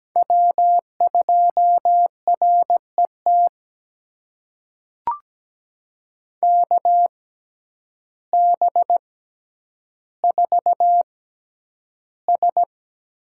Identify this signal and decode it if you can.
Morse code, heard as a keyed tone, about 17 wpm, spaced as for 9 wpm, 700 Hz. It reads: W2RET KB4S
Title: USA Call Signs - 6x Character Spacing - 17wpm